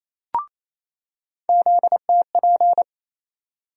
Morse code, heard as a keyed tone, about 28 wpm, 700 Hz. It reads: ZTP
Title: USA Call Sign Postfixes - 3 Letters 28wpm